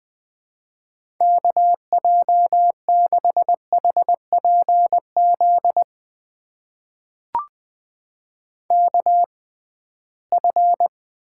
Morse code, heard as a keyed tone, about 20 wpm, spaced as for 15 wpm, 700 Hz. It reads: KJ6HPZ KF
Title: USA Call Signs - 6 Characters - 6x Character Spacing - 20wpm